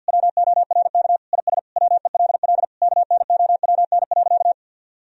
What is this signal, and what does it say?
Morse code, heard as a keyed tone, about 49 wpm, 700 Hz. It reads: WORK IS WELL KNOWN.
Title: Sentences from Top 1000 Words - Review Part 4 - 50wpm